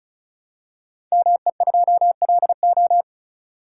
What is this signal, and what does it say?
Morse code, heard as a keyed tone, about 36 wpm, 700 Hz. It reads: ME2LO
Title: Code Groups - 5 Letter:Numbers - 35wpm